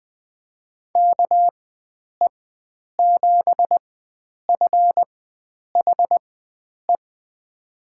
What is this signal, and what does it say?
Morse code, heard as a keyed tone, about 20 wpm, spaced as for 9 wpm, 700 Hz. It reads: KE7FHE
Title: USA Call Signs - 4x Character Spacing - 20wpm